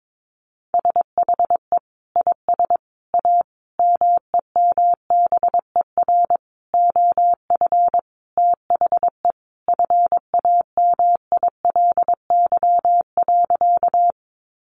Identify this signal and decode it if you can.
Morse code, heard as a keyed tone, about 22 wpm, 700 Hz. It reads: SHE IS A MEMBER OF THE FAMILY.
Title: Sentences from Top 1100 Words 22wpm